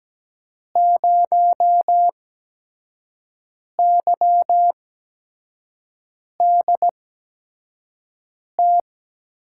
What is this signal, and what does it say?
Morse code, heard as a keyed tone, about 17 wpm, spaced as for 5 wpm, 700 Hz. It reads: 0YDT
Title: USA Call Signs - 6 Characters - 8x Character Spacing - 17wpm